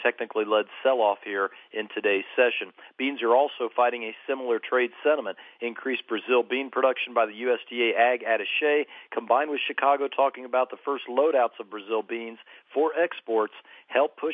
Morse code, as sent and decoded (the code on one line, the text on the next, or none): none